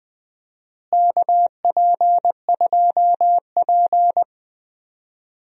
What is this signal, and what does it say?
Morse code, heard as a keyed tone, about 21 wpm, 700 Hz. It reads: KP2P
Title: DXCC-100 Call Signs - 4 Characters - Repeated 2x - 20wpm